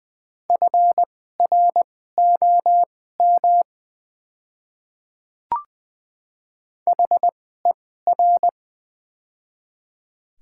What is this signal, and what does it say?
Morse code, heard as a keyed tone, about 20 wpm, spaced as for 14 wpm, 690 Hz. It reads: FROM HER
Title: 2x Character Spacing - Top 100 Words - 20wpm